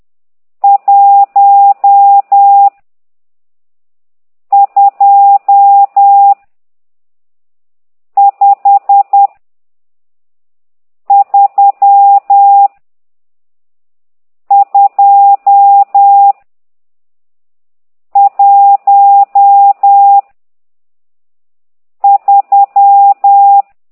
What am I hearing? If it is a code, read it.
Morse code, heard as a keyed tone, about 10 wpm, 800 Hz. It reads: 1 2 5 3 2 1 3
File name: ct2idw / exerc12